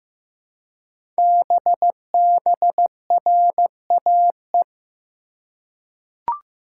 Morse code, heard as a keyed tone, about 15 wpm, 700 Hz. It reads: BBRAE
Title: Code Groups - 5 Letters - 15wpm